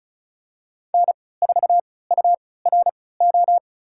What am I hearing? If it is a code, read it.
Morse code, heard as a keyed tone, about 34 wpm, spaced as for 20 wpm, 700 Hz. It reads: N4URO
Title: USA Call Signs - 5 Characters - 3x Character Spacing - 35wpm